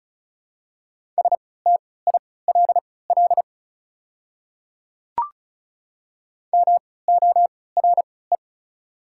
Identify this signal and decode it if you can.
Morse code, heard as a keyed tone, about 34 wpm, spaced as for 20 wpm, 700 Hz. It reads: STILL MORE
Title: 3x Character Spacing - Top 300 Words - 35wpm